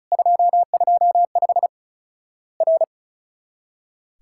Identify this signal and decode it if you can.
Morse code, heard as a keyed tone, about 36 wpm, 700 Hz. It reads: 225 R
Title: Worked All Europe DX Contest - QTC Exchange - 35wpm